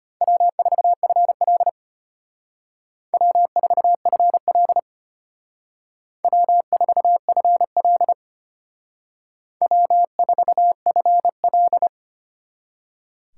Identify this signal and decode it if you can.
Morse code, heard as a keyed tone, about 33 wpm, 700 Hz. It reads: W4FL W4FL W4FL W4FL